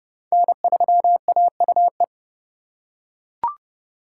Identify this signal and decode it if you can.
Morse code, heard as a keyed tone, about 28 wpm, 700 Hz. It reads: N3AUE